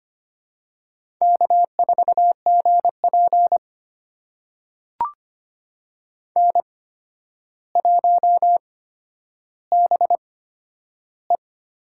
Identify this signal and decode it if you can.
Morse code, heard as a keyed tone, about 25 wpm, spaced as for 11 wpm, 700 Hz. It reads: K4GP N1BE
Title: USA Call Signs - 4 Characters - 8x Character Spacing - 25wpm